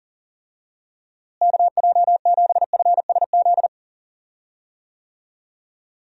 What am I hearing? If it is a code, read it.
Morse code, heard as a keyed tone, about 40 wpm, 700 Hz. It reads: KJ7FSZ
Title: USA Call Signs - 6 Characters - Repeated 2x - 40wpm